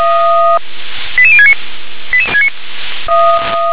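Five-tone selective call (ZVEI), heard as quick runs of five tones, tones of about 70 ms. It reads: 88069
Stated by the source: ZVEI five-tone selective call 88101 received on an SDR.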